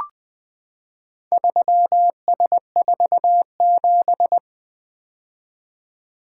Morse code, heard as a keyed tone, about 20 wpm, 700 Hz. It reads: 3S47
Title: Four Letter-Number Combinations - 20wpm